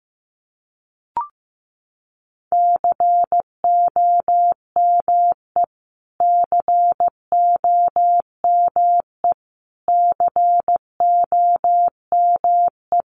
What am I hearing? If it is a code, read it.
Morse code, heard as a keyed tone, about 15 wpm, 700 Hz. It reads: COME COME COME
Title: Top 500 Words Repeated 15wpm